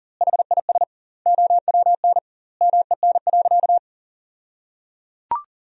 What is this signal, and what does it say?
Morse code, heard as a keyed tone, about 37 wpm, 700 Hz. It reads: HIS OWN MEN.
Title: 2x Word-Spacing - Sentences from Top 300 Words - 40wpm